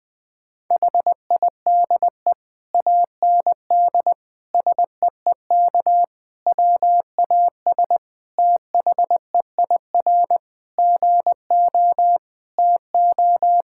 Morse code, heard as a keyed tone, about 20 wpm, 700 Hz. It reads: HIDE AND SEEK WAS THEIR GO TO
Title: Difficult Sentences with Hints - 20wpm